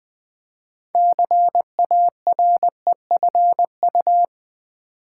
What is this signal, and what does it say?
Morse code, heard as a keyed tone, about 20 wpm, 700 Hz. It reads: CAREFU
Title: Build A Word - 3 Letters At A Time - 9-Letter Words - 20wpm